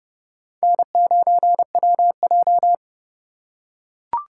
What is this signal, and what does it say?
Morse code, heard as a keyed tone, about 29 wpm, 700 Hz. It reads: N9WJ